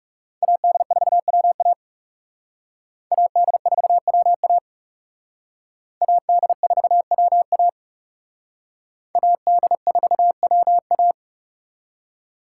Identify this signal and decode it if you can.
Morse code, heard as a keyed tone, about 39 wpm, 700 Hz. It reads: AD4WA AD4WA AD4WA AD4WA